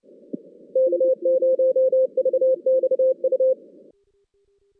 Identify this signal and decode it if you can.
Morse code, heard as a keyed tone, about 30 wpm, 520 Hz. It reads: K0VXU
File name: 2016 JAN CW NAQP